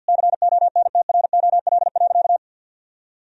Morse code, heard as a keyed tone, about 50 wpm, 700 Hz. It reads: CONTROL.
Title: Sentences from Top 1000 Words - Review Part 4 - 50wpm